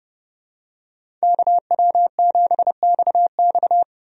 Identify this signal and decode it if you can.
Morse code, heard as a keyed tone, about 30 wpm, 700 Hz. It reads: KW7XX
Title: USA Call Signs - 5 Characters - 30wpm